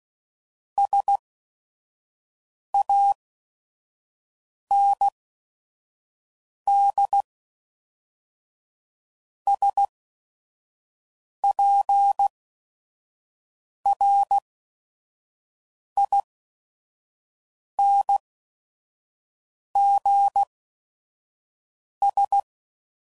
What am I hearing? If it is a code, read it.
Morse code, heard as a keyed tone, about 16 wpm, spaced as for 5 wpm, 800 Hz. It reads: SANDSPRINGS
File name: CW Code Groups @5WPM / fromSM's / Cities